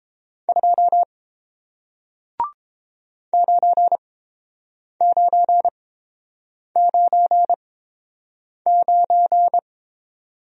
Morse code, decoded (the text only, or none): 2 9999